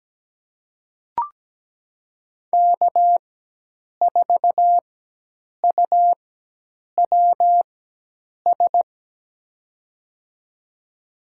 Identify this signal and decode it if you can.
Morse code, heard as a keyed tone, about 17 wpm, spaced as for 8 wpm, 700 Hz. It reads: K4UWS
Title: USA Call Signs - 5 Characters - 4x Character Spacing - 17wpm